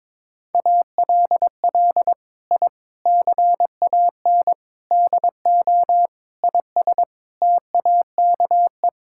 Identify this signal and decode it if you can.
Morse code, heard as a keyed tone, about 21 wpm, 700 Hz. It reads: ALL I CAN DO IS TAKE
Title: Sentences from Top 1000 Words - Review Part 6 - 22wpm